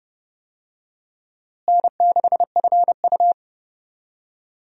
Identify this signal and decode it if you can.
Morse code, heard as a keyed tone, about 30 wpm, 700 Hz. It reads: N6FU
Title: USA Call Signs - 4 Characters - Repeated 3x - 30wpm